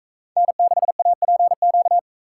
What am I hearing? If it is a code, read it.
Morse code, heard as a keyed tone, about 43 wpm, 700 Hz. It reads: N6APQ